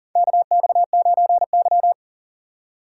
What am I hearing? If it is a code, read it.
Morse code, heard as a keyed tone, about 40 wpm, 700 Hz. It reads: KX9Y
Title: USA Call Signs - 4 Characters - 8x Character Spacing - 40wpm